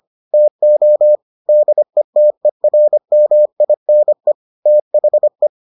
Morse code, heard as a keyed tone, about 25 wpm, 600 Hz. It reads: TO DETERMINE THE